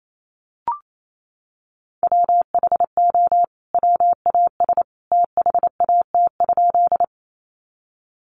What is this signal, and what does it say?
Morse code, heard as a keyed tone, about 28 wpm, 700 Hz. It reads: WHO WAS THAT?
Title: Sentences from Top 100 Words 28wpm